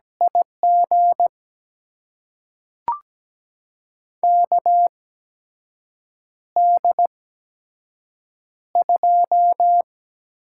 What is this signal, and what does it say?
Morse code, heard as a keyed tone, about 17 wpm, spaced as for 6 wpm, 700 Hz. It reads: IG KD2
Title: USA Call Signs - 6 Characters - 8x Character Spacing - 17wpm